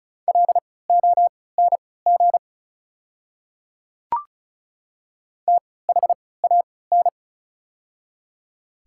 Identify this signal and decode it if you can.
Morse code, heard as a keyed tone, about 34 wpm, spaced as for 20 wpm, 690 Hz. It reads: LONG THAN
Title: 3x Character Spacing - Top 100 Words - 35wpm